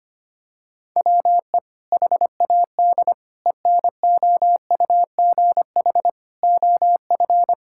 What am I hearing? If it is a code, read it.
Morse code, heard as a keyed tone, about 25 wpm, 700 Hz. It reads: WE HAD ENOUGH OF
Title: Sentences from Top 1600 Words 25wpm